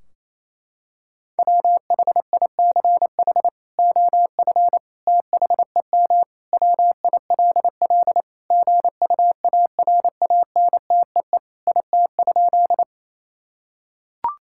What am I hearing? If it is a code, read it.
Morse code, heard as a keyed tone, about 28 wpm, 700 Hz. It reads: WHICH OF THEM WILL GUARANTEE IT?
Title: Sentences from Top 1600 Words 28wpm